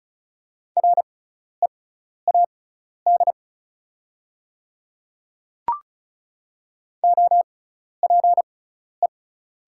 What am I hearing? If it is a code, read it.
Morse code, heard as a keyed tone, about 36 wpm, spaced as for 12 wpm, 700 Hz. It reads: READ OPE
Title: 6x Character Spacing - Top 300 Words - 35wpm